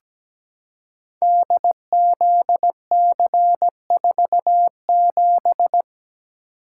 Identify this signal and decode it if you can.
Morse code, heard as a keyed tone, about 17 wpm, 700 Hz. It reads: DZC47
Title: Code Groups - 5 Letter:Numbers - 17wpm